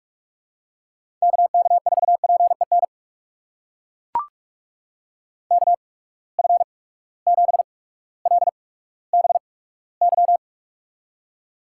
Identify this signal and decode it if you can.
Morse code, heard as a keyed tone, about 42 wpm, spaced as for 19 wpm, 700 Hz. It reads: KK4PEN KF7LBY
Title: USA Call Signs - 6 Characters - 8x Character Spacing - 45wpm